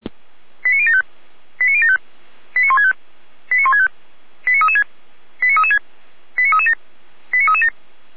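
Five-tone selective call (ZVEI), heard as quick runs of five tones, tones of about 70 ms. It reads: 89075 89075 89165 89165 89336 89336 89337 89337
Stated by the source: ZVEI five-tone call received by SDR.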